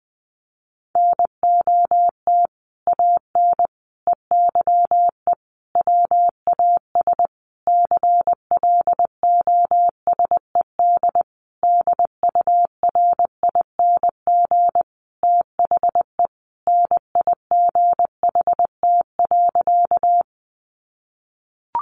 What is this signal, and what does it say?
Morse code, heard as a keyed tone, about 20 wpm, 700 Hz. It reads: NOT AN EYE WAS CLOSED DURING THE NIGHT.
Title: Sentences from Top 1400 Words 20wpm